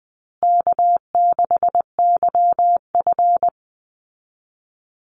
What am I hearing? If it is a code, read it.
Morse code, heard as a keyed tone, about 20 wpm, 700 Hz. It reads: K6YF
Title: Easy Call Signs Repeated 3x - 20wpm